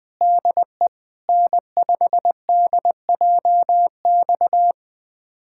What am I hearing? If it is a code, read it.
Morse code, heard as a keyed tone, about 20 wpm, 700 Hz. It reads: DE N5DJX
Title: USA Callsign DE Callsign 20wpm